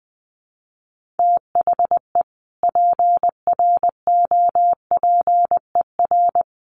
Morse code, heard as a keyed tone, about 20 wpm, 700 Hz. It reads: THE PROPER
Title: Sentences from Top 1400 Words 20wpm